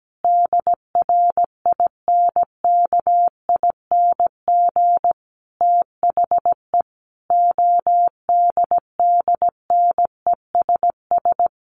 Morse code, heard as a keyed tone, about 17 wpm, 700 Hz. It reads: DRINKING THE ODDNESS
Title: A Sound Of Thunder 17wpm